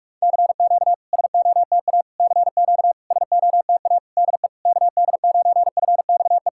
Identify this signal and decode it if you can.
Morse code, heard as a keyed tone, about 46 wpm, 700 Hz. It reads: CQ SOTA CQ SOTA DE KD9FXE
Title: SOTA QSO Practice 45wpm